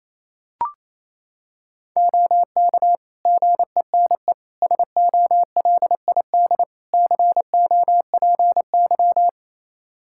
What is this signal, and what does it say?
Morse code, heard as a keyed tone, about 28 wpm, 700 Hz. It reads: OK GENE SOLID COPY
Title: Practice QSO - Exchange 3 - More Info - 28wpm